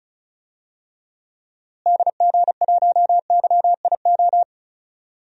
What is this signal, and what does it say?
Morse code, heard as a keyed tone, about 36 wpm, 700 Hz. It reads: DG1YIO
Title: DXCC-100 Call Signs - 6 Characters - Repeated 2x - 35wpm